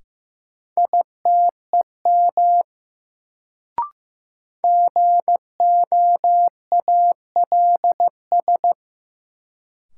Morse code, heard as a keyed tone, about 16 wpm, 700 Hz. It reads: ITEM GOALS
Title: Sentences from Top 1000 Words - Review Part 6 - New Word Drill - 15wpm